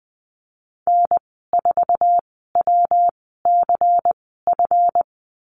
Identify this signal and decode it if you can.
Morse code, heard as a keyed tone, about 20 wpm, spaced as for 14 wpm, 700 Hz. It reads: N4WCF
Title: 238 Lesson 38g - Letter q - Call Signs 20wpm